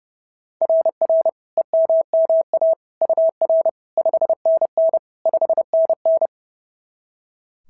Morse code, heard as a keyed tone, about 26 wpm, 650 Hz. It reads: RR EMMA UR 5NN 5NN
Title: SOTA QSO Practice 30wpm